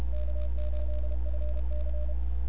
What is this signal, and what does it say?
Morse code, heard as a keyed tone, about 31 wpm, 600 Hz, with noise. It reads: K8FC
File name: n6tr / XE1RCS 1025UTC